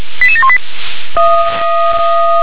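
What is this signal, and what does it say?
Five-tone selective call (ZVEI), heard as quick runs of five tones, tones of about 70 ms. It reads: 88718
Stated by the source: ZVEI five-tone selective call received by SDR.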